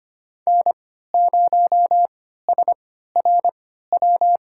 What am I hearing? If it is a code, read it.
Morse code, heard as a keyed tone, about 25 wpm, spaced as for 14 wpm, 700 Hz. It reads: N0SRW
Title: USA Call Signs - 3x Character Spacing - 25wpm